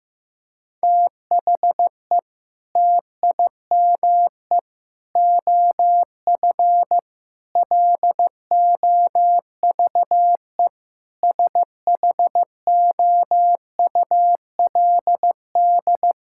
Morse code, heard as a keyed tone, about 15 wpm, 700 Hz. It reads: THE TIME OF LOVE SHOULD
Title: Sentences from Top 1600 Words 15wpm